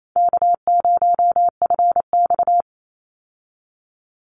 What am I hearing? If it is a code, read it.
Morse code, heard as a keyed tone, about 28 wpm, 700 Hz. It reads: K0FX
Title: Easy Call Signs Repeated 2x - 28wpm